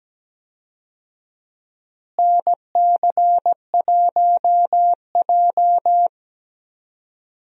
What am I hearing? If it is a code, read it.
Morse code, heard as a keyed tone, about 17 wpm, 700 Hz. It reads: NC1J
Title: USA Call Signs - 4 Characters - Repeated 3x - 17wpm